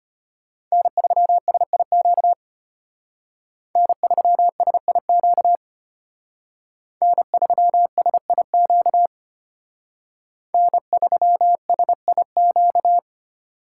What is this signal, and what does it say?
Morse code, heard as a keyed tone, about 32 wpm, 700 Hz. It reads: N3SIQ N3SIQ N3SIQ N3SIQ